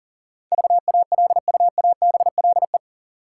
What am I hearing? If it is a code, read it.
Morse code, heard as a keyed tone, about 39 wpm, 700 Hz. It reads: VALUABLE
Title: Sentences from Top 1100 Words - New Word Drill - 40wpm